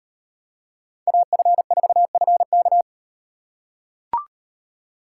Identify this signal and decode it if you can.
Morse code, heard as a keyed tone, about 38 wpm, 700 Hz. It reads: AF4FK